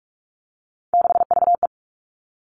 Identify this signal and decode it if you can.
Morse code, heard as a keyed tone, about 46 wpm, 700 Hz. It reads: 6VE
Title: Three Challenging Letter-Number Combinations 45wpm